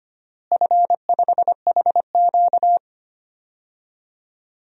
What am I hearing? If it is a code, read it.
Morse code, heard as a keyed tone, about 25 wpm, 700 Hz. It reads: F5HQ
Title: DXCC-100 Call Signs - 4 Characters - Repeated 2x - 25wpm